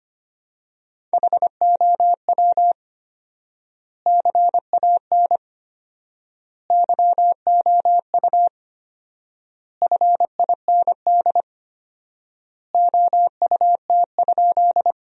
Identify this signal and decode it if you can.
Morse code, heard as a keyed tone, about 25 wpm, 700 Hz. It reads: HOW CAN YOU FIND OUT?
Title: 4x Word-Spacing - Sentences from Top 200 Words - 25wpm